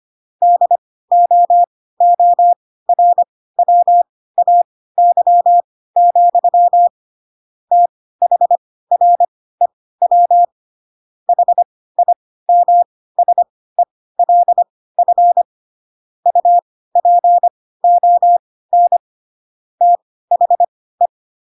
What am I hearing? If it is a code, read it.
Morse code, heard as a keyed tone, about 25 wpm, spaced as for 16 wpm, 700 Hz. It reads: DOORWAY, THREW HIMSELF UPON THE
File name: Pinocchio / Pinocchio ch6 10 25 ik1hge